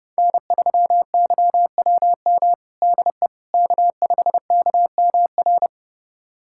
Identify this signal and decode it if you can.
Morse code, heard as a keyed tone, about 30 wpm, 700 Hz. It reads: N3YWM DE K5KMR
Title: USA Callsign DE Callsign - 5 Characters - 30wpm